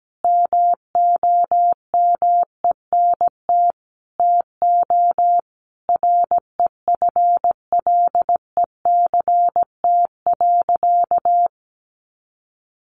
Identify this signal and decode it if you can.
Morse code, heard as a keyed tone, about 17 wpm, 700 Hz. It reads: MOMENT TO REFLECT.
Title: Sentences from Top 1500 Words 17wpm